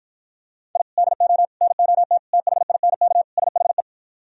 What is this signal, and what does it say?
Morse code, heard as a keyed tone, about 52 wpm, 700 Hz. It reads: I DO NOT THINK SHE